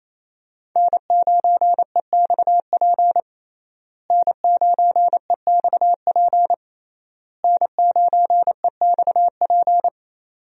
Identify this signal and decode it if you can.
Morse code, heard as a keyed tone, about 28 wpm, 700 Hz. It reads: N9EXP N9EXP N9EXP
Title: USA Call Signs - 5 Characters - Repeated 3x - 28wpm